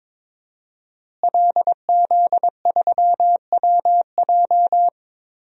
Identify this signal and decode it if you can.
Morse code, heard as a keyed tone, about 22 wpm, 700 Hz. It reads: LZ3WJ
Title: DXCC-100 Call Signs - 5 Characters - Repeated 2x - 22wpm